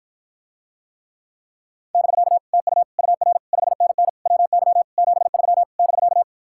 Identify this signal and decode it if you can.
Morse code, heard as a keyed tone, about 53 wpm, 700 Hz. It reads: <BK> TU UR 5NN WY 73 <BK>